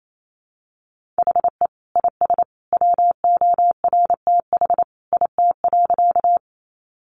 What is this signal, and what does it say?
Morse code, heard as a keyed tone, about 28 wpm, 700 Hz. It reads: HE IS WORTH IT.
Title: Sentences from Top 800 Words 28wpm